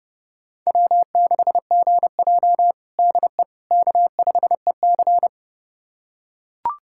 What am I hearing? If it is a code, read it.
Morse code, heard as a keyed tone, about 30 wpm, 700 Hz. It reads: W6GJ DE K5EC
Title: USA Callsign DE Callsign - 4 Characters - 30wpm